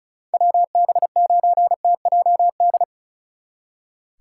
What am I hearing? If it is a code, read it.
Morse code, heard as a keyed tone, about 43 wpm, 700 Hz. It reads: WB9TJD
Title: Speed Racing - USA Call Signs 35wpm